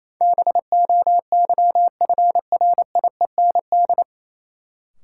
Morse code, heard as a keyed tone, about 28 wpm, 700 Hz. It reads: BOYFRIEND
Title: Sentences from Top 1000 Words - Review Part 5 - New Word Drill - 28wpm